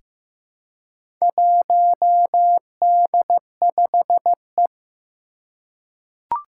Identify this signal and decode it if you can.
Morse code, heard as a keyed tone, about 15 wpm, 700 Hz. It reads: 1D5E
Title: Four Challenging Letter-Number Combinations 15wpm